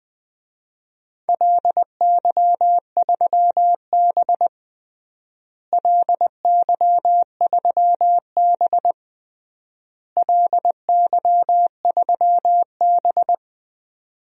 Morse code, heard as a keyed tone, about 20 wpm, 700 Hz. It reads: LY3B LY3B LY3B
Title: DXCC-100 Call Signs - 4 Characters - Repeated 3x - 20wpm